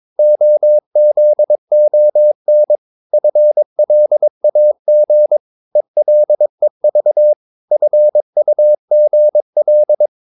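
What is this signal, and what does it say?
Morse code, heard as a keyed tone, about 22 wpm, 600 Hz. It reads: OZON FLAG ELEV FUGL
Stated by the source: Morse practice of short Danish words at 22 wpm, from CW med Gnister.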